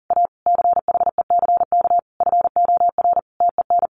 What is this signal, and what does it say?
Morse code, heard as a keyed tone, about 40 wpm, 700 Hz. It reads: A CHECK FOR TEN
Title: A Sound Of Thunder 40wpm